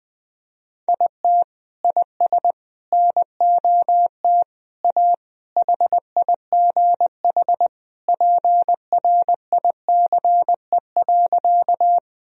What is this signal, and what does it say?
Morse code, heard as a keyed tone, about 20 wpm, 700 Hz. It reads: IT IS NOT A HIGH PRICE.
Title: Sentences from Top 1000 Words - Review Part 4 - 20wpm